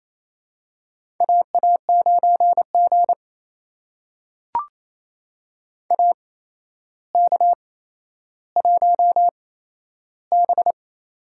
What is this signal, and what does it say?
Morse code, heard as a keyed tone, about 28 wpm, spaced as for 12 wpm, 700 Hz. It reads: AA9G AK1B
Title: USA Call Signs - 4 Characters - 8x Character Spacing - 28wpm